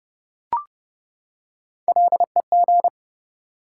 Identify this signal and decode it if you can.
Morse code, heard as a keyed tone, about 31 wpm, 700 Hz. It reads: LEG
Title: Three Letter Words - 30wpm